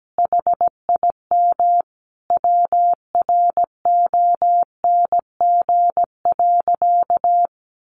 Morse code, heard as a keyed tone, about 17 wpm, 700 Hz. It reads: HIM WRONG.
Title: Sentences from Top 1300 Words 17wpm